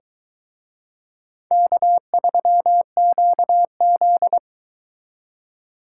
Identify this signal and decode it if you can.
Morse code, heard as a keyed tone, about 22 wpm, 700 Hz. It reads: K3QZ